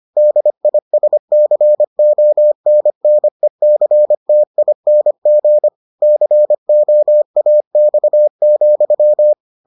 Morse code, heard as a keyed tone, about 25 wpm, 600 Hz. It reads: DISCONNECTING COAX,